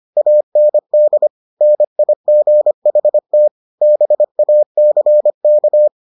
Morse code, heard as a keyed tone, about 25 wpm, 600 Hz. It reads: AND NIGHT BACK